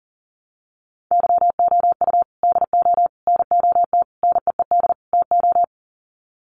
Morse code, heard as a keyed tone, about 40 wpm, 700 Hz. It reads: YOU DO NOT NEED TO
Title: Sets of 5 Words 40wpm